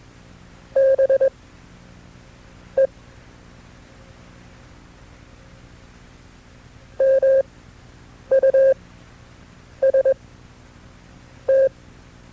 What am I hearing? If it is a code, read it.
Morse code, heard as a keyed tone, about 21 wpm, spaced as for 6 wpm, 540 Hz. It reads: BE MUST
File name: FreeFiles / CW-practice / KB6MT-tape2-side2